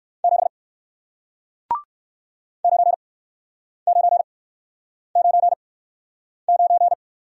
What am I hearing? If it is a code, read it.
Morse code, heard as a keyed tone, about 61 wpm, spaced as for 9 wpm, 700 Hz. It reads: 7 9999